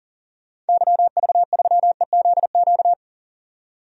Morse code, heard as a keyed tone, about 39 wpm, 700 Hz. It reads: YV3EZQ
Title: DXCC-100 Call Signs - 6 Characters - 40wpm